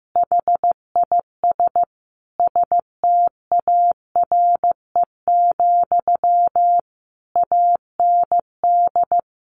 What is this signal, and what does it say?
Morse code, heard as a keyed tone, about 15 wpm, 700 Hz. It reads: HIS STARE, AND
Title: A Sound Of Thunder 15wpm